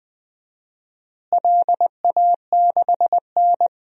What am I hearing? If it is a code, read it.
Morse code, heard as a keyed tone, about 20 wpm, 700 Hz. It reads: LA6N
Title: DXCC-100 Call Signs - 4 Characters - Repeated 3x - 20wpm